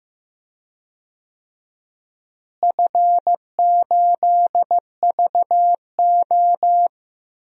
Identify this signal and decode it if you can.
Morse code, heard as a keyed tone, about 16 wpm, 700 Hz. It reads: F8VO
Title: DXCC-100 Call Signs - 4 Characters - Repeated 3x - 15wpm